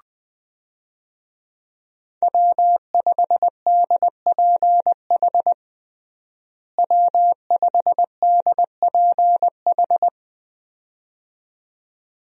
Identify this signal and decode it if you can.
Morse code, heard as a keyed tone, about 20 wpm, 700 Hz. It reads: W5DPH W5DPH
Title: USA Call Signs - 5 Characters - Repeated 2x - 20wpm